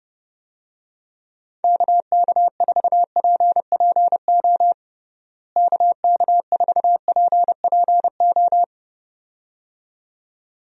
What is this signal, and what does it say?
Morse code, heard as a keyed tone, about 30 wpm, 700 Hz. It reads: KK4PPO KK4PPO
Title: USA Call Signs - 6 Characters - Repeated 2x - 30wpm